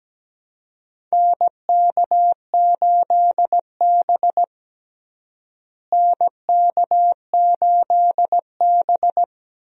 Morse code, heard as a keyed tone, about 17 wpm, 700 Hz. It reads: NK8B NK8B
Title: USA Call Signs - 4 Characters - Repeated 2x - 17wpm